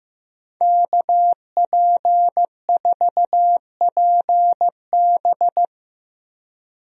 Morse code, heard as a keyed tone, about 15 wpm, 700 Hz. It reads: KP4PB
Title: DXCC-100 Call Signs - 5 Characters - 15wpm